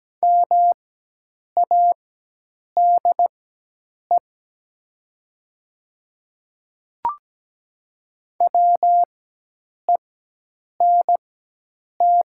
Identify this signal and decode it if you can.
Morse code, heard as a keyed tone, about 17 wpm, spaced as for 8 wpm, 700 Hz. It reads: MADE WENT
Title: 4x Character Spacing - Top 300 Words - 17wpm